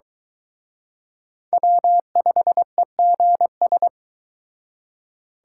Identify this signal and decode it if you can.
Morse code, heard as a keyed tone, about 23 wpm, 700 Hz. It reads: W5EGS